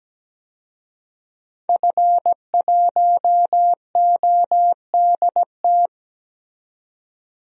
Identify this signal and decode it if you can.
Morse code, heard as a keyed tone, about 17 wpm, 700 Hz. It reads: F1ODT
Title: DXCC-100 Call Signs - 5 Characters - 17wpm